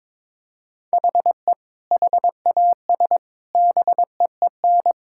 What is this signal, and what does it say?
Morse code, heard as a keyed tone, about 21 wpm, 700 Hz. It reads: HE HAS BEEN
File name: practice / 22wpm / Sentences with 3-Word Phrases - 22wpm